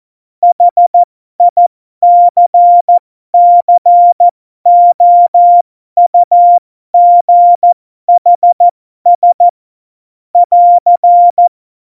Morse code, heard as a keyed tone, about 14 wpm, spaced as for 12 wpm, 700 Hz. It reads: HICCOUGHS <AR>
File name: Morsekurs / Witze